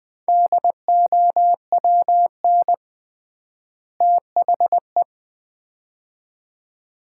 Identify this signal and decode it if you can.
Morse code, heard as a keyed tone, about 20 wpm, 700 Hz. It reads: DOWN THE
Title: 3x Word-Spacing - Sets of 2 Words - 20wpm